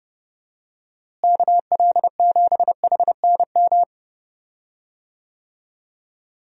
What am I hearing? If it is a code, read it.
Morse code, heard as a keyed tone, about 30 wpm, 700 Hz. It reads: KL7HNM
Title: USA Call Signs - 6 Characters - Repeated 3x - 30wpm